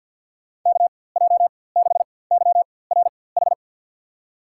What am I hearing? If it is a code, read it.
Morse code, heard as a keyed tone, about 53 wpm, spaced as for 24 wpm, 700 Hz. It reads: KJ6YRH
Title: USA Call Signs - 6 Characters - 4x Character Spacing - 50wpm